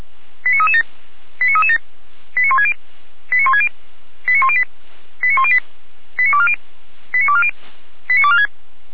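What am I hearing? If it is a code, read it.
Five-tone selective call (ZVEI), heard as quick runs of five tones, tones of about 70 ms. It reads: 89337 89337 89160 89160 89108 89108 89240 89240 89246
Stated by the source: Received by SDR on a ZVEI selective-call channel.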